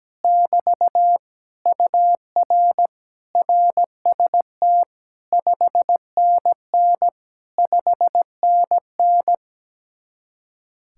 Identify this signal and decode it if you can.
Morse code, heard as a keyed tone, about 17 wpm, 700 Hz. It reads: <BT> UR RST 5NN 5NN
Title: Speed-Racing Common QSO Phrases Encore - 17wpm